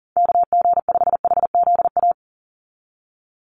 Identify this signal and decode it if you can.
Morse code, heard as a keyed tone, about 40 wpm, 700 Hz. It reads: KG5HZA
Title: USA Call Signs 40wpm